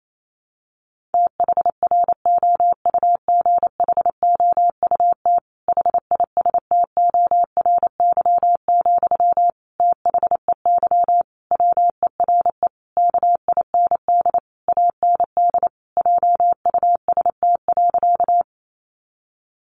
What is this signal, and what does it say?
Morse code, heard as a keyed tone, about 28 wpm, 700 Hz. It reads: THROUGHOUT HISTORY, THEY WERE KIND AND JUST.
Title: Sentences from Top 800 Words 28wpm